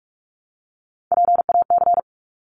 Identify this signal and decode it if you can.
Morse code, heard as a keyed tone, about 44 wpm, 700 Hz. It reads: PAC
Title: ARRL Sections 45wpm